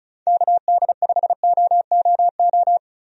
Morse code, heard as a keyed tone, about 34 wpm, 700 Hz. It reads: KD5OOO
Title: USA Call Signs - 6 Characters - 35wpm